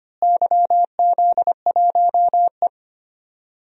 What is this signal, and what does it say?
Morse code, heard as a keyed tone, about 25 wpm, 700 Hz. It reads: YZ1E
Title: DXCC-100 Call Signs - 4 Characters - Repeated 3x - 25wpm